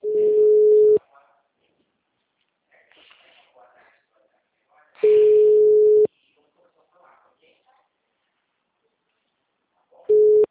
Ringback tone (Europe, 425 Hz).